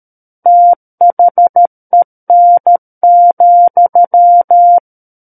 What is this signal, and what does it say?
Morse code, heard as a keyed tone, about 13 wpm, 700 Hz. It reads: THEN,
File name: Book One / 04-Chapter 4 - 13 WPM